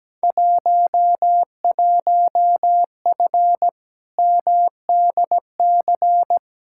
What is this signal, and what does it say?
Morse code, heard as a keyed tone, about 17 wpm, 700 Hz. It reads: 11F MDC
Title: Field Day Exchanges Encore - 17wpm